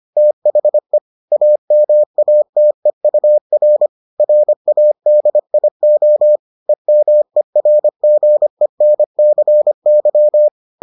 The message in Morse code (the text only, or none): THE AMATEUR RADIO EMERGENCY